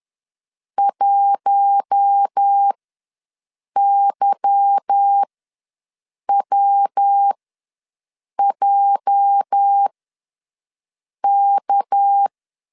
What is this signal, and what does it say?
Morse code, heard as a keyed tone, about 10 wpm, spaced as for 6 wpm, 800 Hz. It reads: 1YWJK
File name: LEZIONI / 8 YJ45 / 8 YJ ws 10wpm 4043